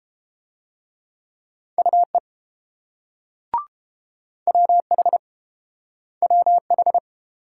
Morse code, heard as a keyed tone, about 31 wpm, 700 Hz. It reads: UE WH WH